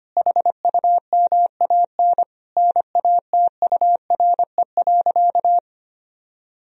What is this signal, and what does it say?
Morse code, heard as a keyed tone, about 25 wpm, 700 Hz. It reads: HUMAN NATURE.
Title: Sentences from Top 1000 Words - Review Part 3 - 25wpm